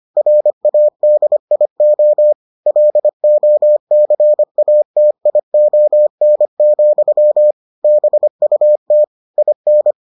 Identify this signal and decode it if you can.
Morse code, heard as a keyed tone, about 25 wpm, 600 Hz. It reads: RADIO LOCATION, BUT IN